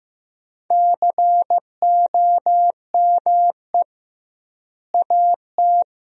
Morse code, heard as a keyed tone, about 15 wpm, 700 Hz. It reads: COME AT
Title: 2x Word-Spacing - Sentences from Top 300 Words - 15wpm